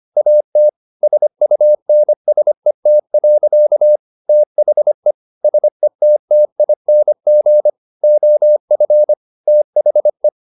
Morse code, heard as a keyed tone, about 25 wpm, 600 Hz. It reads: AT SUNSET. THE SETTING OF THE